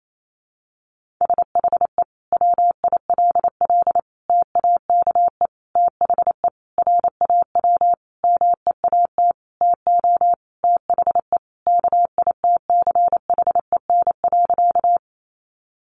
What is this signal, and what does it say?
Morse code, heard as a keyed tone, about 28 wpm, 700 Hz. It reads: SHE WILL TAKE THE RAW MEAT TO THE KITCHEN.
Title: Sentences from Top 1500 Words 28wpm